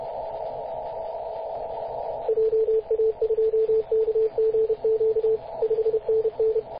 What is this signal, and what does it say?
Morse code, heard as a keyed tone, about 31 wpm, 440 Hz, with radio noise. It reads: JA2KGQ 5NN
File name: posts / 2016-mar-19-0241z-vk9ck-21-032mhz-cw